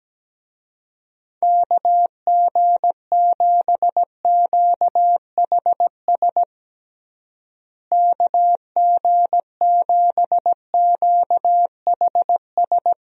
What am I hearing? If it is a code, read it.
Morse code, heard as a keyed tone, about 17 wpm, 700 Hz. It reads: KG7QHS KG7QHS
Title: USA Call Signs - 6 Characters - Repeated 2x - 17wpm